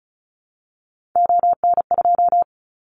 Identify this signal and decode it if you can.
Morse code, heard as a keyed tone, about 36 wpm, 700 Hz. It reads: ON2
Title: Callsign Prefixes for Top 50 DXCC - 35wpm